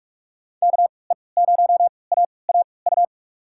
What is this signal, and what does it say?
Morse code, heard as a keyed tone, about 44 wpm, spaced as for 25 wpm, 700 Hz. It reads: KE0AAU
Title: USA Call Signs - 3x Character Spacing - 45wpm